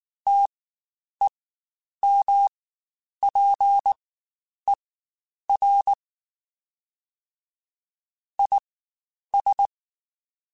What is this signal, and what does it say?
Morse code, heard as a keyed tone, about 19 wpm, spaced as for 9 wpm, 800 Hz. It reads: TEMPER IS
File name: Quotes Spaced / Quotes Spaced-mp3s / Temper sp @18